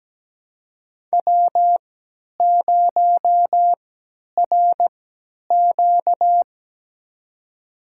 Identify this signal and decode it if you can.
Morse code, heard as a keyed tone, about 17 wpm, spaced as for 10 wpm, 700 Hz. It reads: W0RQ
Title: USA Call Signs - 4 Characters - 3x Character Spacing - 17wpm